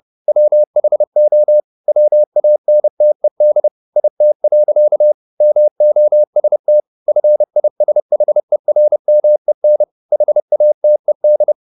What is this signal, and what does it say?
Morse code, heard as a keyed tone, about 30 wpm, 600 Hz. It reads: WHO WANTED IT. MOST FISHERMEN HATED
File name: the old man and the sea / the old man and the sea 1 30wpm